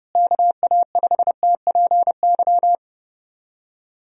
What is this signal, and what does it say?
Morse code, heard as a keyed tone, about 30 wpm, 700 Hz. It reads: KA5TPY
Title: USA Call Signs - 6 Characters - Repeated 3x - 30wpm